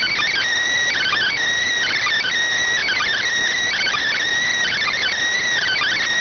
Emergency Alert System (SAME) tones, heard as harsh buzzing bursts. It is heard from 0.9 to 5.1 s.